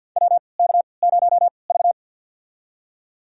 Morse code, heard as a keyed tone, about 51 wpm, spaced as for 28 wpm, 700 Hz. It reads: WK0V
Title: USA Call Signs - 4 Characters - 3x Character Spacing - 50wpm